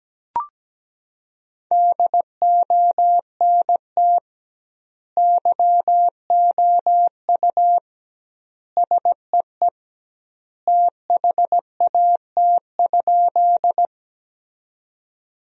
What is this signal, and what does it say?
Morse code, heard as a keyed tone, about 17 wpm, 700 Hz. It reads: DONT YOU SEE THAT?
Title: 2x Word-Spacing - Sentences from Top 300 Words - 17wpm